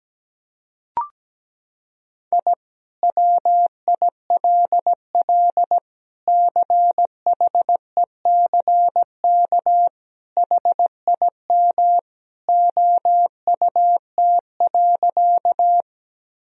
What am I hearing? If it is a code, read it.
Morse code, heard as a keyed tone, about 17 wpm, 700 Hz. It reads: I WILL CHECK HIM OUT.
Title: Sentences from Top 1000 Words - Review Part 3 - 17wpm